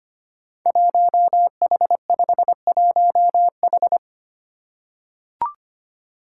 Morse code, heard as a keyed tone, about 25 wpm, 700 Hz. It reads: 1H51H
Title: Five Challenging Letter-Number Combinations 25wpm